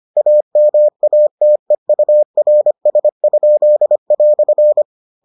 Morse code, heard as a keyed tone, about 25 wpm, 600 Hz. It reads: AMATEURS?"